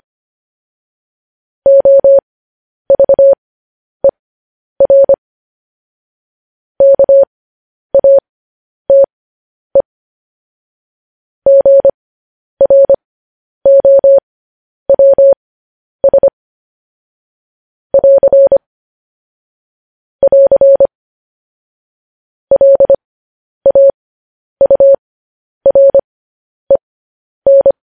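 Morse code, heard as a keyed tone, about 25 wpm, spaced as for 10 wpm, 550 Hz. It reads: OVER KATE GROWS <AR> <AR> LAUREN
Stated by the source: Morse news headlines at 10.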